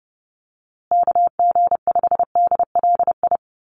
Morse code, heard as a keyed tone, about 30 wpm, 700 Hz. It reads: KG5DLI
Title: USA Call Signs Repeated 3x 30wpm